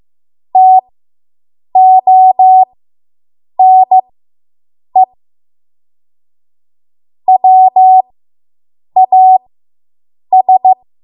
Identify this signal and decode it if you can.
Morse code, heard as a keyed tone, about 15 wpm, spaced as for 7 wpm, 750 Hz. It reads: TONE WAS